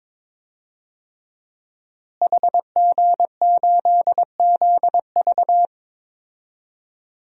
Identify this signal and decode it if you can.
Morse code, heard as a keyed tone, about 22 wpm, 700 Hz. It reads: HG8ZV
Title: DXCC-100 Call Signs - 5 Characters - Repeated 3x - 22wpm